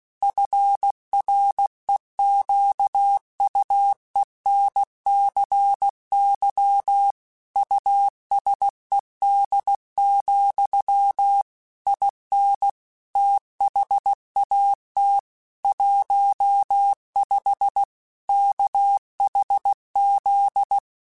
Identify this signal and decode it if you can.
Morse code, heard as a keyed tone, about 16 wpm, 800 Hz. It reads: FREQUENCY USED, IN THAT 15 KHZ
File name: Technical Subjects / Technical Subjects-mp3s / Sw Pwr Sup @15